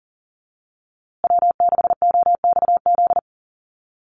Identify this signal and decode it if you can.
Morse code, heard as a keyed tone, about 40 wpm, 700 Hz. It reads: W6OXZ
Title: USA Call Signs Repeated 2x 40wpm